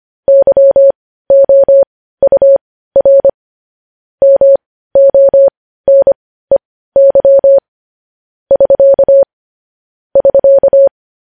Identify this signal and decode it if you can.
Morse code, heard as a keyed tone, about 25 wpm, spaced as for 15 wpm, 550 Hz. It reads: YOUR MONEY <SK> <SK>
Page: news headlines in morse code at 15 wpm